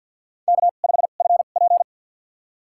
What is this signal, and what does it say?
Morse code, heard as a keyed tone, about 51 wpm, spaced as for 36 wpm, 700 Hz. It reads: K5FP
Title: USA Call Signs - 4 Characters - 2x Character Spacing - 50wpm